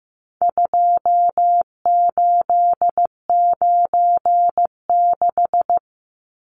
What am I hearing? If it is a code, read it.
Morse code, heard as a keyed tone, about 15 wpm, 700 Hz. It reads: 2896
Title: Numbers - 4-Digit 15wpm